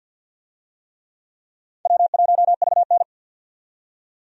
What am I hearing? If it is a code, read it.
Morse code, heard as a keyed tone, about 52 wpm, 700 Hz. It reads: W1VN
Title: USA Call Signs - 4 Characters - 50wpm